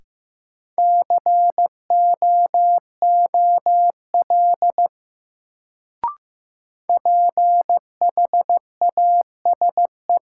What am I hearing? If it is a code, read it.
Morse code, heard as a keyed tone, about 15 wpm, 700 Hz. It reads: COOL PHASE
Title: Sentences from Top 1200 Words - New Word Drill - 15wpm